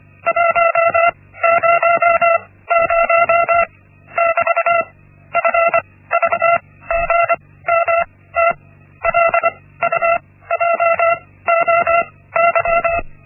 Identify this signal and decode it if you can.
Morse code, heard as a keyed tone, about 24 wpm, spaced as for 16 wpm, 680 Hz. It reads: 100 <BT> FVGMT LUJOY
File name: Linksmakalnis-1